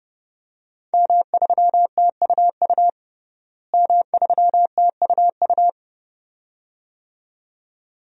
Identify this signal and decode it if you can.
Morse code, heard as a keyed tone, about 30 wpm, 700 Hz. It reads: M3TUU M3TUU
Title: DXCC-100 Call Signs - 5 Characters - Repeated 2x - 30wpm